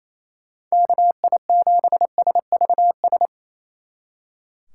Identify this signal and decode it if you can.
Morse code, heard as a keyed tone, about 33 wpm, 700 Hz. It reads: KI7SVS
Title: Speed Racing - USA 2x3 Call Signs 28wpm